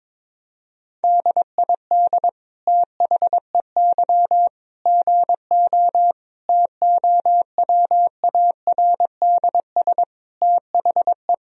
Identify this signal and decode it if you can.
Morse code, heard as a keyed tone, about 22 wpm, 700 Hz. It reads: DID THEY GO TOWARDS THE
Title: Sentences from Top 1000 Words - Review Part 2 - 22wpm